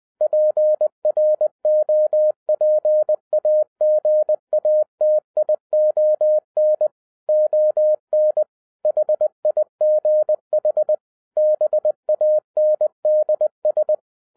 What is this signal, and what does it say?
Morse code, heard as a keyed tone, about 20 wpm, 610 Hz. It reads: PROPAGATION ON HIGH BANDS
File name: 2012-01-mp3 / n15-INVESTIGATION INTO THE POSSIBILITY OF WINNING PREFIX VERSION 3 AWARD-20wpm